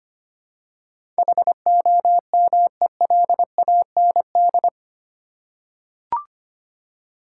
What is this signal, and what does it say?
Morse code, heard as a keyed tone, about 25 wpm, 690 Hz. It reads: HOMELAND
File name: 8 Letter Words - First 3 Letters Given - 25wpm